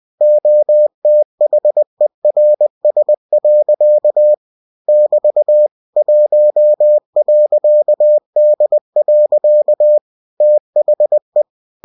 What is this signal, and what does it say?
Morse code, heard as a keyed tone, about 20 wpm, 600 Hz. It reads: OTHERS. <BT> 1.D. THE
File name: cw-books / winnie-the-pooh / winnie-20wpm-page-0114